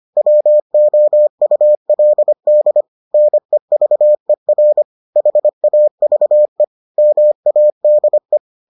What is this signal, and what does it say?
Morse code, heard as a keyed tone, about 25 wpm, 600 Hz. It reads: WOULD NEVER HAVE MADE